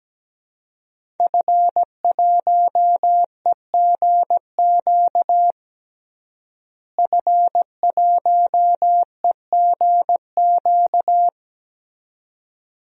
Morse code, heard as a keyed tone, about 17 wpm, 700 Hz. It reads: F1EGQ F1EGQ
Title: DXCC-100 Call Signs - 5 Characters - Repeated 2x - 17wpm